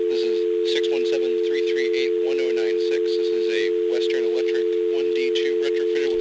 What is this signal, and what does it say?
Dial tone (North America), 350 + 440 Hz, with speech.
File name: blackratchet / recordings / charles-st-payphone